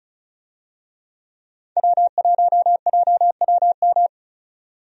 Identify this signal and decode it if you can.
Morse code, heard as a keyed tone, about 36 wpm, 700 Hz. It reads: W1JWM
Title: USA Call Signs - 5 Characters - 35wpm